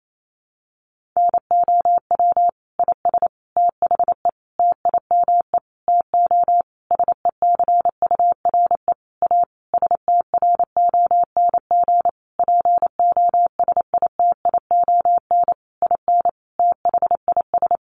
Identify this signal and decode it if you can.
Morse code, heard as a keyed tone, about 28 wpm, 700 Hz. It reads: NOW IS THE TIME TO SECURE A STRONG POSITION IN THIS
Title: Sentences from Top 1200 Words 28wpm